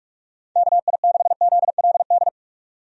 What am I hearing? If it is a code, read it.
Morse code, heard as a keyed tone, about 46 wpm, 700 Hz. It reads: KI6ZLD
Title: USA Call Signs - 6 Characters - 45wpm